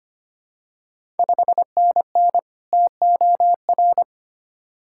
Morse code, heard as a keyed tone, about 25 wpm, 700 Hz. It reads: 5NN TOR
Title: Ontario QSO Party Exchange 25wpm